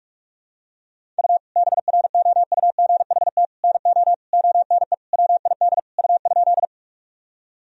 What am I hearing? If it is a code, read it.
Morse code, heard as a keyed tone, about 46 wpm, 700 Hz. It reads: U BROUGHT NO ONE WID U?
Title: Sentences from Top 500 Words and Abbreviations 45wpm